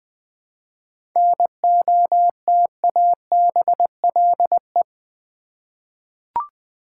Morse code, heard as a keyed tone, about 20 wpm, 700 Hz. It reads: NOTABLE
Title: 7 Letter Words - First 3 Letters Given - 20wpm